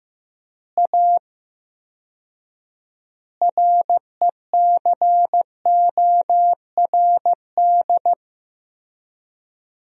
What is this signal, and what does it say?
Morse code, heard as a keyed tone, about 16 wpm, 700 Hz. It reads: A RECORD
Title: 4x Word-Spacing - Sentences from Top 500 Words - 15wpm